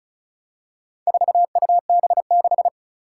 Morse code, heard as a keyed tone, about 36 wpm, 700 Hz. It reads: 4UB6
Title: Four Challenging Letter-Number Combinations 35wpm